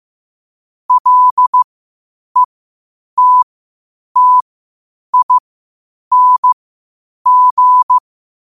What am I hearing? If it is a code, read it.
Morse code, heard as a keyed tone, about 15 wpm, spaced as for 8 wpm, 1000 Hz. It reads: LETTING
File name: 8wpm